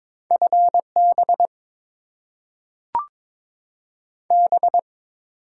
Morse code, heard as a keyed tone, about 22 wpm, 700 Hz. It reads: FB B
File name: Top 20 QSO Elements - 8x Character Spacing - 22wpm